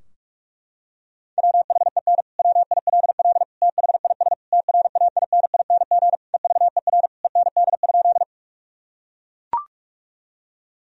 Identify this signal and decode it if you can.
Morse code, heard as a keyed tone, about 41 wpm, 700 Hz. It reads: WHEN WILL THIS TRAINING EVER END?
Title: Sentences from Top 1000 Words - Review Part 4 - 45wpm